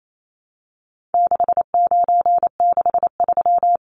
Morse code, heard as a keyed tone, about 28 wpm, 700 Hz. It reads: 6963
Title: Numbers - 4-Digit 28wpm